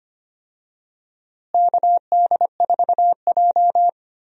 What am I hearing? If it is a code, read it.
Morse code, heard as a keyed tone, about 25 wpm, 700 Hz. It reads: KD4J
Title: USA Call Signs - 4 Characters - Repeated 2x - 25wpm